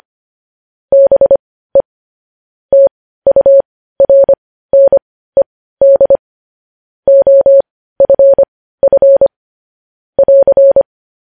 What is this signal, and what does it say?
Morse code, heard as a keyed tone, about 25 wpm, spaced as for 15 wpm, 550 Hz. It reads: BE TURNED OFF <AR>
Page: news headlines in morse code at 15 wpm